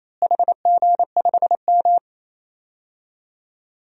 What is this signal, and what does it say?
Morse code, heard as a keyed tone, about 28 wpm, 700 Hz. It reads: HG5M
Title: DXCC-100 Call Signs - 4 Characters - Repeated 2x - 28wpm